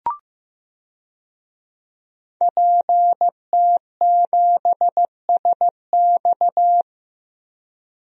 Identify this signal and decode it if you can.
Morse code, heard as a keyed tone, about 16 wpm, 700 Hz. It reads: PT7SX
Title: DXCC-100 Call Signs - 5 Characters - Repeated 2x - 15wpm